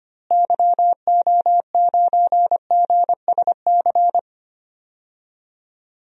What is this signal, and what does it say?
Morse code, heard as a keyed tone, about 25 wpm, 700 Hz. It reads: YO9GSC
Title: DXCC-100 Call Signs - 6 Characters - Repeated 3x - 25wpm